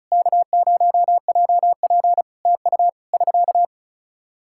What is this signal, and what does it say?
Morse code, heard as a keyed tone, about 36 wpm, 700 Hz. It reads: K0JP TU <SK>
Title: Practice QSO - Exchange 4 - Ending - 35wpm